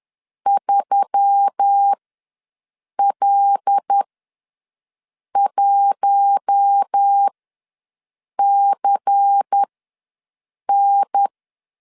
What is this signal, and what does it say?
Morse code, heard as a keyed tone, about 10 wpm, spaced as for 6 wpm, 800 Hz. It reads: 3L1CN
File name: LEZIONI / 5 LBVC / 5 LBVC ws 10wpm 3733